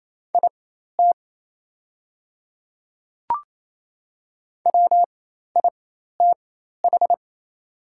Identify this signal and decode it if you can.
Morse code, heard as a keyed tone, about 27 wpm, spaced as for 13 wpm, 690 Hz. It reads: IT WITH
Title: 4x Character Spacing - Top 200 Words - 28wpm